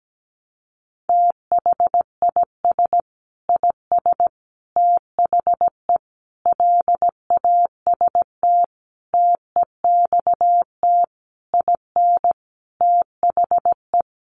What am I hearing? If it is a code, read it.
Morse code, heard as a keyed tone, about 17 wpm, 700 Hz. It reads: THIS IS THE LAST TEXT IN THE
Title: Sentences from Top 1200 Words 17wpm